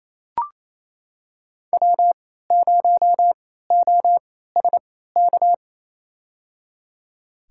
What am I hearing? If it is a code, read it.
Morse code, heard as a keyed tone, about 28 wpm, spaced as for 16 wpm, 700 Hz. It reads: W0OSK
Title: USA Call Signs - 3x Character Spacing - 28wpm